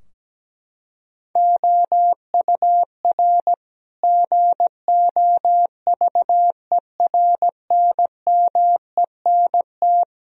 Morse code, heard as a keyed tone, about 17 wpm, 700 Hz. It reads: OUR GOVERNMENT
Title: Sentences from Top 1000 Words - Review Part 4 - 17wpm